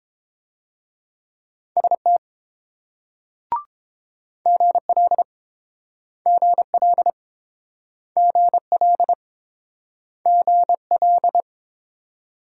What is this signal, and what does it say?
Morse code, heard as a keyed tone, about 29 wpm, 700 Hz. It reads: ST GL GL GL GL